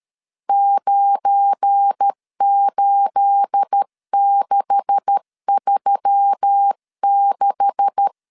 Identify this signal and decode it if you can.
Morse code, heard as a keyed tone, about 13 wpm, 800 Hz. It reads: 98636
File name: LEZIONI / 9 286 12wpm 2305